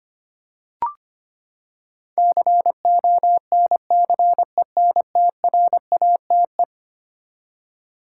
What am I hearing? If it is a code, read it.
Morse code, heard as a keyed tone, about 25 wpm, 700 Hz. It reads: CONCENTRATE
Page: morse-code-ninja-practice-25wpm